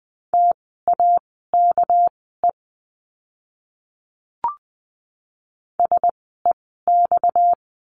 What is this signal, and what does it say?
Morse code, heard as a keyed tone, about 20 wpm, spaced as for 14 wpm, 700 Hz. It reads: TAKE SEX
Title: 230 Lesson 37f - Letter x - Words 20wpm